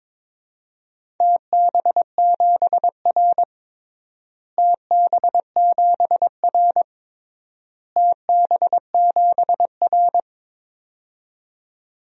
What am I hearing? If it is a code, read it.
Morse code, heard as a keyed tone, about 22 wpm, 700 Hz. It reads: TB7R TB7R TB7R
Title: DXCC-100 Call Signs - 4 Characters - Repeated 3x - 22wpm